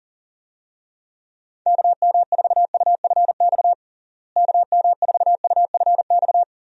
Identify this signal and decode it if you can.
Morse code, heard as a keyed tone, about 40 wpm, 700 Hz. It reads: KM4UFX KM4UFX
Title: USA Call Signs - 6 Characters - Repeated 2x - 40wpm